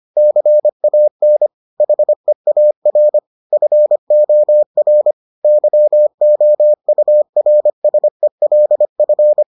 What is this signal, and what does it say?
Morse code, heard as a keyed tone, about 25 wpm, 600 Hz. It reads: CAN HEAR FOR YOURSELF